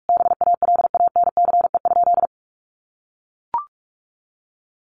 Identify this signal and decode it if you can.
Morse code, heard as a keyed tone, about 44 wpm, 700 Hz. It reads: BALANCE?
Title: Sentences from Top 900 Words 45wpm